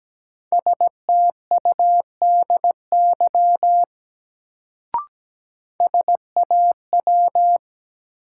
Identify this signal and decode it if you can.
Morse code, heard as a keyed tone, about 17 wpm, 700 Hz. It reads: STUDY SAW
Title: Sentences from Top 300 Words - New Word Drill - 17wpm